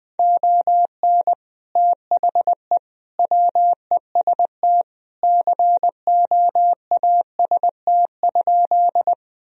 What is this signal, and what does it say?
Morse code, heard as a keyed tone, about 20 wpm, 700 Hz. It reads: ON THE WEST COAST?
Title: Sentences from Top 1000 Words - Review Part 4 - 20wpm